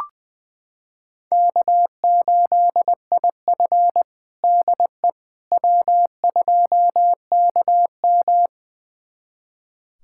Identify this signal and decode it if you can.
Morse code, heard as a keyed tone, about 21 wpm, 700 Hz. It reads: K8IF DE W2KM
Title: USA Callsign DE Callsign - 4 Characters - 20wpm